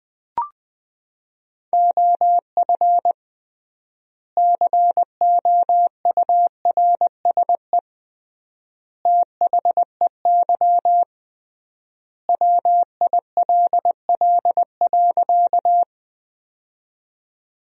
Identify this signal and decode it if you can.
Morse code, heard as a keyed tone, about 20 wpm, 700 Hz. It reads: OF COURSE THEY WILL.
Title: 3x Word-Spacing - Sentences from Top 500 Words - 20wpm